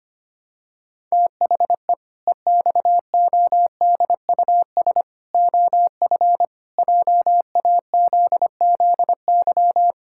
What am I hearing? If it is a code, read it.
Morse code, heard as a keyed tone, about 25 wpm, 700 Hz. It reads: THE EXODUS OF JAZZY
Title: Pangram Challenge v2 - 25wpm